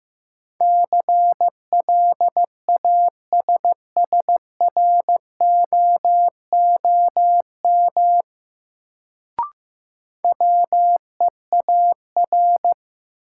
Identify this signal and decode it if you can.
Morse code, heard as a keyed tone, about 15 wpm, 700 Hz. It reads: CLASSROOM WEAR
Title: Sentences from Top 1500 Words - New Word Drill Alt - 15wpm